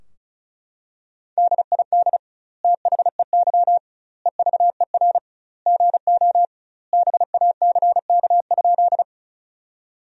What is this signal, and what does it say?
Morse code, heard as a keyed tone, about 34 wpm, 700 Hz. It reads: DID THEY EVER GO BACK?
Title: 2x Word-Spacing - Sentences from Top 400 Words - 35wpm